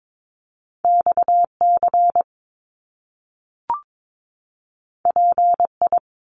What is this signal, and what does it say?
Morse code, heard as a keyed tone, about 22 wpm, 700 Hz. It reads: XC PI
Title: Top Two Letter Combinations in Words 22wpm